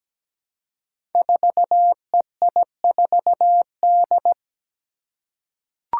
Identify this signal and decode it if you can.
Morse code, heard as a keyed tone, about 17 wpm, 700 Hz. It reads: 4EI4D
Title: Five Challenging Letter-Number Combinations 17wpm